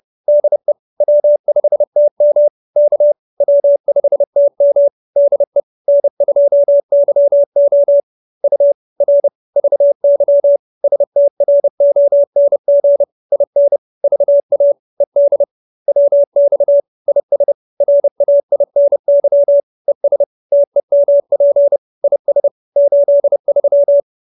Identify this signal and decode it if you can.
Morse code, heard as a keyed tone, about 30 wpm, 570 Hz. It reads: DE W5TM K W5TM DE N2YO U R VY STRONG IN VA ED WX IS RAINY ES TEMP IS 83
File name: QSO316 30